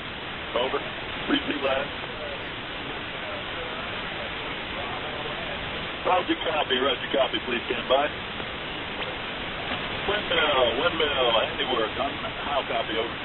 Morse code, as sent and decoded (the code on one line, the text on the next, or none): none